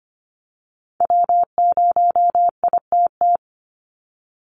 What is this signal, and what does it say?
Morse code, heard as a keyed tone, about 25 wpm, 700 Hz. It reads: W0ITT
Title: Call Signs Repeated 2x - 25wpm